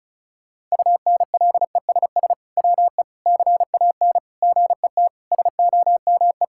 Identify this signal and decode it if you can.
Morse code, heard as a keyed tone, about 36 wpm, 700 Hz. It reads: UNLESS WE CAN GET SOME
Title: Sentences from Top 1000 Words - Review Part 6 - 35wpm